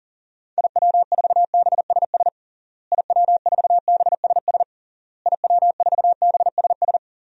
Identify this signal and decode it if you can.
Morse code, heard as a keyed tone, about 40 wpm, 700 Hz. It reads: IW4BSS IW4BSS IW4BSS
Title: DXCC-100 Call Signs - 6 Characters - Repeated 3x - 40wpm